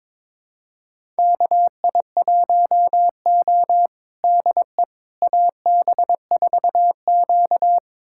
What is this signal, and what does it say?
Morse code, heard as a keyed tone, about 22 wpm, 700 Hz. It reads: KI1O DE AB4Q
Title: USA Callsign DE Callsign - 4 Characters - 22wpm